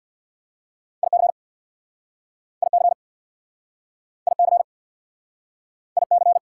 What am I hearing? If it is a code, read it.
Morse code, heard as a keyed tone, about 66 wpm, 700 Hz. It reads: IC IC IC IC